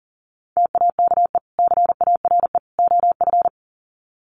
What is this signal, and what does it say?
Morse code, heard as a keyed tone, about 40 wpm, 700 Hz. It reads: TAKE CARE OF
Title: Sets of 3 Words 40wpm